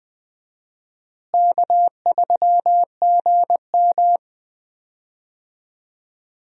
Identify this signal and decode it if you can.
Morse code, heard as a keyed tone, about 20 wpm, 700 Hz. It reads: K3GM
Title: USA Call Signs - 4 Characters - Repeated 2x - 20wpm